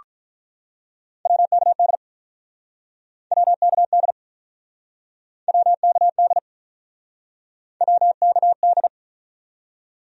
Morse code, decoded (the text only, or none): WKD WKD WKD WKD